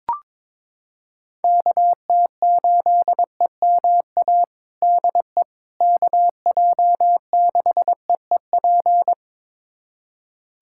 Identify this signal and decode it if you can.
Morse code, heard as a keyed tone, about 21 wpm, 700 Hz. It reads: KT8EMA DE KJ6EEP
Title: USA Callsign DE Callsign 22wpm